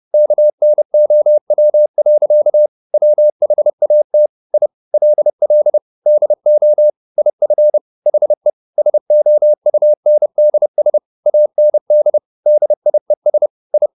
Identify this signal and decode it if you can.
Morse code, heard as a keyed tone, about 30 wpm, 600 Hz. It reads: KNOW. WHAT I LL DO IF HE SOUNDS AND DIES I
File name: the old man and the sea / the old man and the sea 1 30wpm